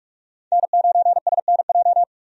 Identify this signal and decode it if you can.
Morse code, heard as a keyed tone, about 41 wpm, 700 Hz. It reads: N9SNJ
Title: USA Call Signs - 5 Characters - 45wpm